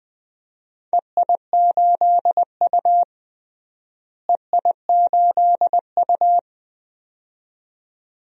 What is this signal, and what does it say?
Morse code, heard as a keyed tone, about 20 wpm, 700 Hz. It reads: EI8U EI8U
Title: DXCC-100 Call Signs - 4 Characters - Repeated 2x - 20wpm